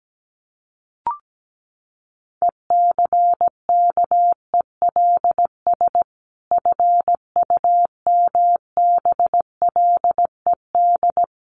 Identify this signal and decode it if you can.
Morse code, heard as a keyed tone, about 17 wpm, 700 Hz. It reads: ECKELS FUMBLED
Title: A Sound Of Thunder 17wpm